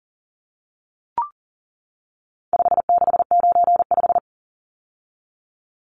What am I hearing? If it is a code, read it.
Morse code, heard as a keyed tone, about 40 wpm, 700 Hz. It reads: 5695
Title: Numbers - 4-Digit 40wpm